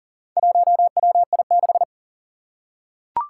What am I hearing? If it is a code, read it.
Morse code, heard as a keyed tone, about 40 wpm, 700 Hz. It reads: 1WI6
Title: Four Challenging Letter-Number Combinations 40wpm